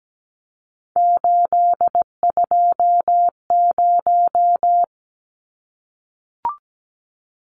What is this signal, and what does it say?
Morse code, heard as a keyed tone, about 17 wpm, 700 Hz. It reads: 820
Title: Numbers - 3-Digit 17wpm